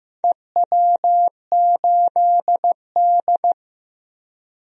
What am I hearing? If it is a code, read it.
Morse code, heard as a keyed tone, about 15 wpm, 700 Hz. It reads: EW8D
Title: DXCC-100 Call Signs - 4 Characters - Repeated 2x - 15wpm